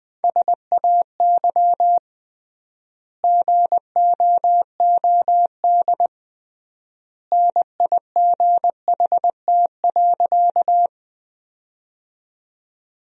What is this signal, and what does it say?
Morse code, heard as a keyed tone, about 20 wpm, 700 Hz. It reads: SAY GOOD NIGHT.
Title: 3x Word-Spacing - Sentences from Top 300 Words - 20wpm